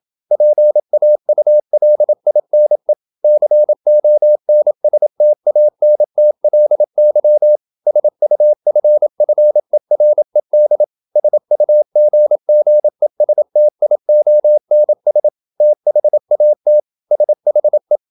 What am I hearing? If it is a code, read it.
Morse code, heard as a keyed tone, about 27 wpm, 600 Hz. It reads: PAULINE CONSTANTLY SUFFERED SUGGESTIONS THAT SHE
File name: Never 0027wpm